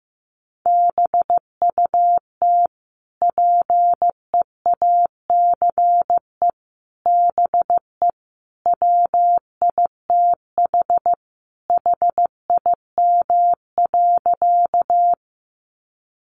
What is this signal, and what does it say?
Morse code, heard as a keyed tone, about 15 wpm, 700 Hz. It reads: BUT PEACE BE WITH HIM.
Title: Sentences from Top 1500 Words 15wpm